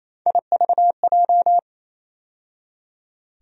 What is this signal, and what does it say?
Morse code, heard as a keyed tone, about 28 wpm, 700 Hz. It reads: IVJ
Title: USA Call Sign Postfixes - 3 Letters 28wpm